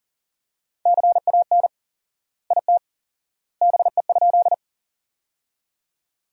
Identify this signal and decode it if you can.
Morse code, heard as a keyed tone, about 40 wpm, 700 Hz. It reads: CAN IT BE?
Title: 4x Word-Spacing - Sentences from Top 100 Words - 40wpm